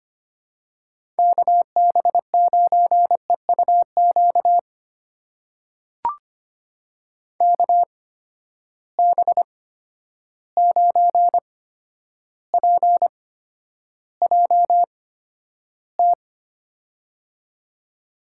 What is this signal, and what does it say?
Morse code, heard as a keyed tone, about 25 wpm, spaced as for 11 wpm, 700 Hz. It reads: KB9EUQ KB9PJT
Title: USA Call Signs - 6 Characters - 8x Character Spacing - 25wpm